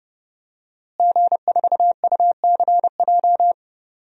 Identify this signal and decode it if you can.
Morse code, heard as a keyed tone, about 30 wpm, 700 Hz. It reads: G4UCJ
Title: DXCC-100 Call Signs - 5 Characters - Repeated 2x - 30wpm